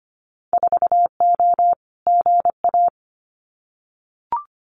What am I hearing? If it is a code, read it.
Morse code, heard as a keyed tone, about 25 wpm, 700 Hz. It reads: 4O GA
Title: Winter Field Day Exchanges 25wpm